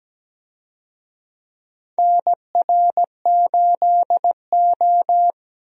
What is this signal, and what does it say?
Morse code, heard as a keyed tone, about 17 wpm, 700 Hz. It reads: NR8O
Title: USA Call Signs - 4 Characters - 17wpm